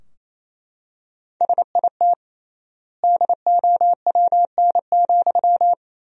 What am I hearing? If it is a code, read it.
Morse code, heard as a keyed tone, about 28 wpm, 700 Hz. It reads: SIT DOWN,
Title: 3x Word-Spacing - Sentences from Top 500 Words - 28wpm